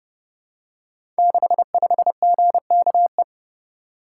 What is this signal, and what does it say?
Morse code, heard as a keyed tone, about 28 wpm, 700 Hz. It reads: 65GKE
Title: Code Groups - 5 Letter:Numbers - 30wpm